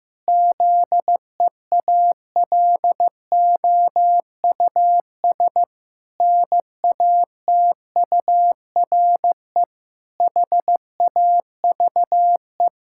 Morse code, heard as a keyed tone, about 15 wpm, 700 Hz. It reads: ZEALOUS NATURE HAVE
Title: Pangram Challenge v2 - 15wpm